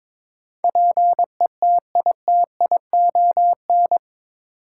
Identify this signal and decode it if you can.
Morse code, heard as a keyed tone, about 21 wpm, 700 Hz. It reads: PETITION
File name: practice / 22wpm / Eight Letter Words - 22wpm